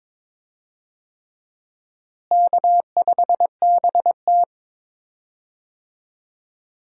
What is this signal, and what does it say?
Morse code, heard as a keyed tone, about 22 wpm, 700 Hz. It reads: K5BT
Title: USA Call Signs - 4 Characters - Repeated 3x - 22wpm